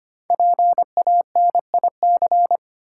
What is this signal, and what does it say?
Morse code, heard as a keyed tone, about 25 wpm, 700 Hz. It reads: PANIC
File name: 5 Letter Words - First 3 Letters Given - 25wpm